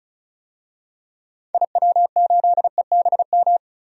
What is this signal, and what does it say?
Morse code, heard as a keyed tone, about 35 wpm, 700 Hz. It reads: IW8EBM
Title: DXCC-100 Call Signs - 6 Characters - 35wpm